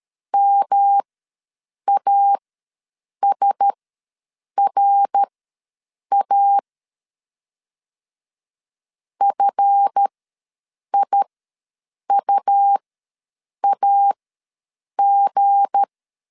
Morse code, heard as a keyed tone, about 13 wpm, spaced as for 7 wpm, 800 Hz. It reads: MASRA FIUAG
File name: LEZIONI / 7 GF9Z / 7 GF ws 12wpm 3140